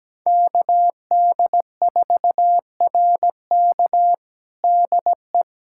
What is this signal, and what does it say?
Morse code, heard as a keyed tone, about 17 wpm, 700 Hz. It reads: KD4RK DE
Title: USA Callsign DE Callsign 17wpm